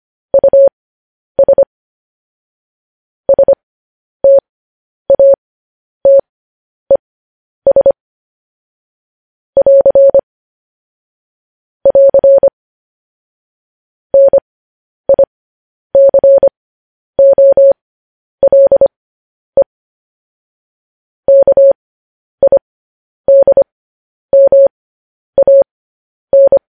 Morse code, heard as a keyed tone, about 25 wpm, spaced as for 10 wpm, 550 Hz. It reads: US STATES <AR> <AR> NICOLE KIDMAN